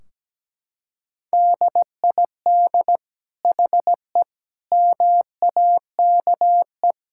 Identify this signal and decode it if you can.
Morse code, heard as a keyed tone, about 17 wpm, 700 Hz. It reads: DID HE MAKE
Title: Sentences from Top 1600 Words 17wpm